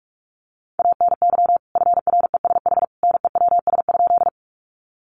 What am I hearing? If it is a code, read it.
Morse code, heard as a keyed tone, about 44 wpm, 700 Hz. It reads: ANY FRESH NEWS?
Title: Sentences from Top 1200 Words 45wpm